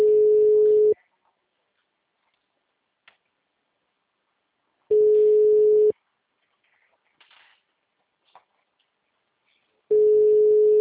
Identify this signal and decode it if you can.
Ringback tone (Europe, 425 Hz), with speech.
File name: out- -2001- - - .293